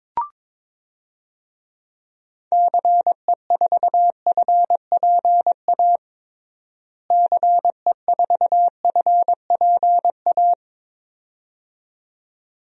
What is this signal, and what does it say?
Morse code, heard as a keyed tone, about 22 wpm, 700 Hz. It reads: CE4FPA CE4FPA
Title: DXCC-100 Call Signs - 6 Characters - Repeated 2x - 22wpm